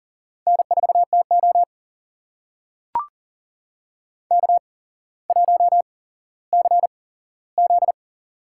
Morse code, heard as a keyed tone, about 39 wpm, spaced as for 17 wpm, 700 Hz. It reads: N4TO K1CZ
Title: USA Call Signs - 4 Characters - 8x Character Spacing - 40wpm